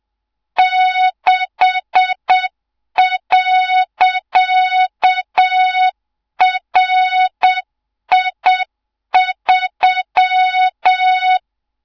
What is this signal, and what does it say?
Morse code, heard as a keyed tone, about 7 wpm, 3700 Hz, with noise. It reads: 6.RI3
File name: Morze-HA2MN / hang / 24 GYAKORLAT